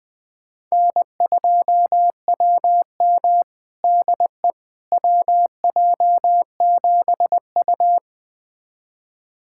Morse code, heard as a keyed tone, about 20 wpm, 700 Hz. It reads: N2WM DE WJ7U
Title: USA Callsign DE Callsign - 4 Characters - 20wpm